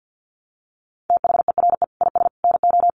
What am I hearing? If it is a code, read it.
Morse code, heard as a keyed tone, about 50 wpm, 700 Hz. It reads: THERE IS NO
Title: Sets of 5 Words Encore 50wpm